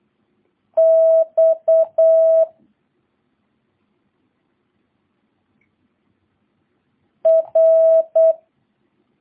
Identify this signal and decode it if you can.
Morse code, heard as a keyed tone, about 8 wpm, 650 Hz. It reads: X R